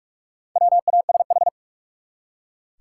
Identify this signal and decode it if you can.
Morse code, heard as a keyed tone, about 42 wpm, 700 Hz. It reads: WASH
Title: Four Letter Words - 45wpm